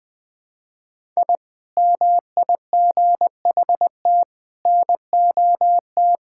Morse code, heard as a keyed tone, about 19 wpm, 700 Hz. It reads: I MIGHT NOT
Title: Sentences with 4-Word Phrases - 20wpm